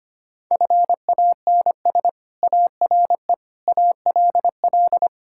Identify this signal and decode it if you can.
Morse code, heard as a keyed tone, about 25 wpm, 700 Hz. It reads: FANS ARE ALL
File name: practice / 25wpm / Sentences with 4-Word Phrases - 25wpm